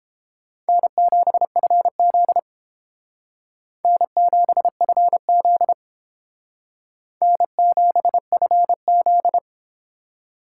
Morse code, heard as a keyed tone, about 30 wpm, 700 Hz. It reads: N7FZ N7FZ N7FZ